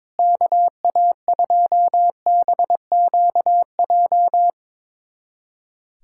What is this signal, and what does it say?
Morse code, heard as a keyed tone, about 22 wpm, 700 Hz. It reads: KA2BQJ
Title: USA Callsign DE Callsign - 6 Characters - 22wpm